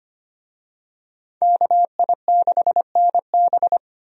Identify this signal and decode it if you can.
Morse code, heard as a keyed tone, about 25 wpm, 700 Hz. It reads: KI6NB
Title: USA Call Signs - 5 Characters - 25wpm